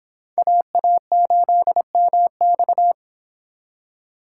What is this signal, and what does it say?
Morse code, heard as a keyed tone, about 25 wpm, 700 Hz. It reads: AA8MX